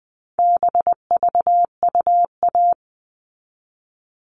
Morse code, heard as a keyed tone, about 20 wpm, 700 Hz. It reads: BVUA
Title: All Cut Numbers - 4-Digit 20wpm